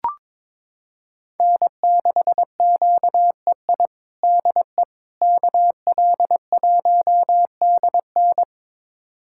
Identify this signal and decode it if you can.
Morse code, heard as a keyed tone, about 22 wpm, 700 Hz. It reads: N6QEI DE KL1DN
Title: USA Callsign DE Callsign - 5 Characters - 22wpm